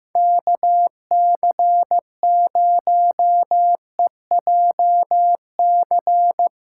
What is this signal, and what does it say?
Morse code, heard as a keyed tone, about 17 wpm, 700 Hz. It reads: KC0EJC
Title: Speed Racing - USA 2x3 Call Signs 15wpm